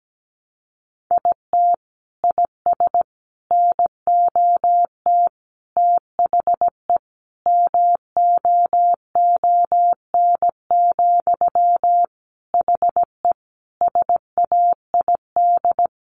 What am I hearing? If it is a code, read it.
Morse code, heard as a keyed tone, about 17 wpm, 700 Hz. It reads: IT IS NOT THE MOON, HE SAID
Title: Sentences from Top 1100 Words 17wpm